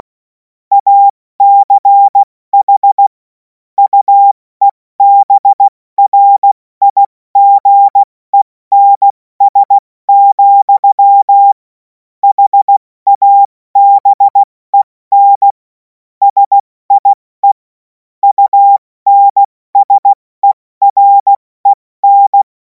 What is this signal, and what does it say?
Morse code, heard as a keyed tone, about 16 wpm, spaced as for 14 wpm, 800 Hz. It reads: ACH UEBRIGENS, HABEN SIE UNSEREN
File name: Morsekurs / Witze / Witz 10. März 2026 70 F80